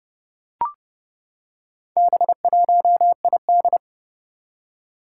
Morse code, heard as a keyed tone, about 30 wpm, 690 Hz. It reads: B1ID
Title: Four Challenging Letter-Number Combinations 30wpm